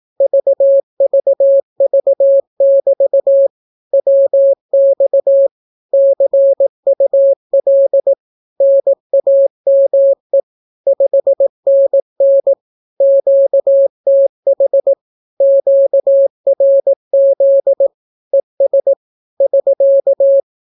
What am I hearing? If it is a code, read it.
Morse code, heard as a keyed tone, about 18 wpm, 550 Hz. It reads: VVV<BT> WX CUL NAME 5NN QTH QRZ ES <SK>
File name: naukapodstaw / skroty / skroty